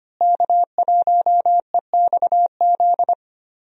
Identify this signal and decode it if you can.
Morse code, heard as a keyed tone, about 25 wpm, 700 Hz. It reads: K1EXZ
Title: USA Call Signs - 5 Characters - Repeated 3x - 25wpm